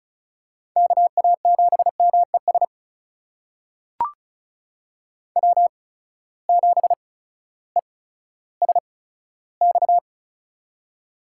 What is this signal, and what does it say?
Morse code, heard as a keyed tone, about 36 wpm, spaced as for 16 wpm, 700 Hz. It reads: KA7MES W7ESX
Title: USA Call Signs - 8x Character Spacing - 35wpm